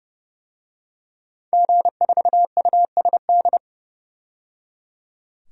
Morse code, heard as a keyed tone, about 30 wpm, 700 Hz. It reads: G4USD
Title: DXCC-100 Call Signs - 5 Characters - 30wpm